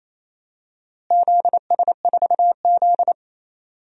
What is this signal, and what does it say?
Morse code, heard as a keyed tone, about 28 wpm, 700 Hz. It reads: ZS4Z
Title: DXCC-100 Call Signs - 4 Characters - Repeated 3x - 28wpm